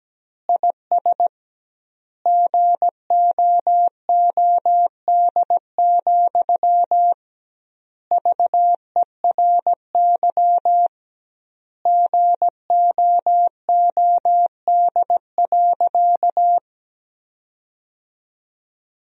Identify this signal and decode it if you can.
Morse code, heard as a keyed tone, about 17 wpm, 700 Hz. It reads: IS GOOD, VERY GOOD.
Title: 2x Word-Spacing - Sentences from Top 200 Words - 17wpm